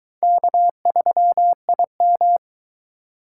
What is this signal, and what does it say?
Morse code, heard as a keyed tone, about 21 wpm, 700 Hz. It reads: K3IM